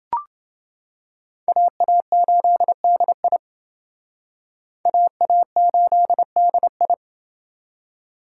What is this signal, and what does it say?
Morse code, heard as a keyed tone, about 28 wpm, 700 Hz. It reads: AA8DI AA8DI